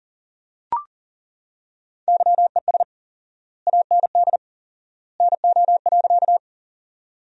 Morse code, heard as a keyed tone, about 40 wpm, 700 Hz. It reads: YES AND NO.
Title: 4x Word-Spacing - Sentences from Top 500 Words - 40wpm